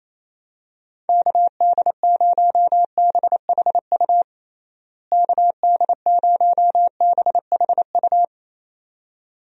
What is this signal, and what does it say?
Morse code, heard as a keyed tone, about 28 wpm, 700 Hz. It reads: KD0BHU KD0BHU
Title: USA Call Signs - 6 Characters - Repeated 2x - 28wpm